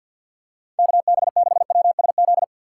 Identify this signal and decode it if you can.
Morse code, heard as a keyed tone, about 52 wpm, 700 Hz. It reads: KB6WSZ
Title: USA Call Signs - 6 Characters - 3x Character Spacing - 50wpm